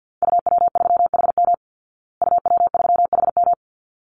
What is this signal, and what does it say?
Morse code, heard as a keyed tone, about 48 wpm, 700 Hz. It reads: UW3HM UW3HM
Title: Call Signs Repeated 3x - 50wpm